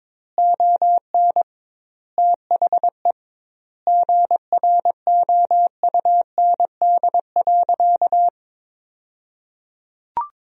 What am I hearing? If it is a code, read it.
Morse code, heard as a keyed tone, about 21 wpm, 700 Hz. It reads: ON THE GROUND.
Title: 2x Word-Spacing - Sentences from Top 500 Words - 22wpm